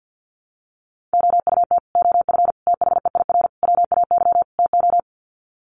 Morse code, heard as a keyed tone, about 50 wpm, 700 Hz. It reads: OUT OF THEIR WAY TO
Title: Sets of 5 Words Encore2 50wpm